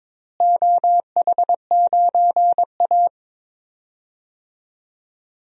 Morse code, heard as a keyed tone, about 22 wpm, 700 Hz. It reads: OH9A
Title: DXCC-100 Call Signs - 4 Characters - Repeated 2x - 22wpm